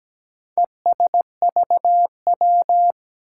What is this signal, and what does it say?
Morse code, heard as a keyed tone, about 17 wpm, 700 Hz. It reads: ESVW
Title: Four Challenging Letter-Number Combinations 17wpm